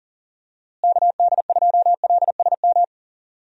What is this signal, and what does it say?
Morse code, heard as a keyed tone, about 40 wpm, 700 Hz. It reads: KD2LSM
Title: USA Call Signs - 6 Characters - Repeated 3x - 40wpm